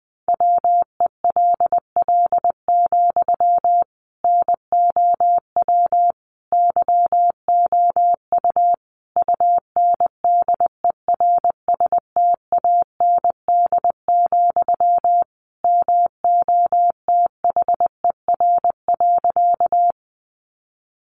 Morse code, heard as a keyed tone, about 20 wpm, 700 Hz. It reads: WELL, NOW YOU UNDERSTAND, MOTHER.
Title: Sentences from Top 600 Words 20wpm